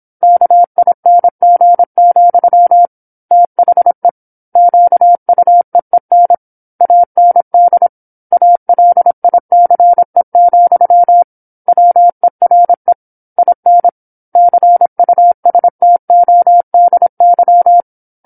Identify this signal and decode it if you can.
Morse code, heard as a keyed tone, about 27 wpm, 700 Hz. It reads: KING, THE QUEEN AND ALICE, WERE IN CUSTODY
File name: 08-Chapter-08 26 WPM